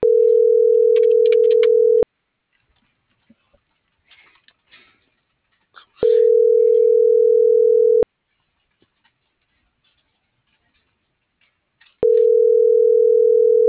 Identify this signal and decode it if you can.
Ringback tone (North America), 440 + 480 Hz.